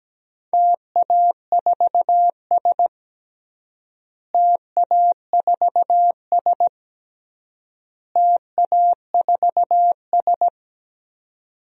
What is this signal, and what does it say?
Morse code, heard as a keyed tone, about 17 wpm, 700 Hz. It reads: TA4S TA4S TA4S
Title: DXCC-100 Call Signs - 4 Characters - Repeated 3x - 17wpm